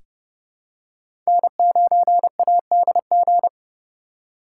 Morse code, heard as a keyed tone, about 30 wpm, 700 Hz. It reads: N9ADG
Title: K1USN SST QSO Practice - 30wpm